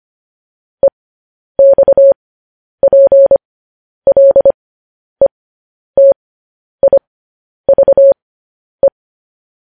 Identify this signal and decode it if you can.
Morse code, heard as a keyed tone, about 25 wpm, spaced as for 10 wpm, 550 Hz. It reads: EXPLETIVE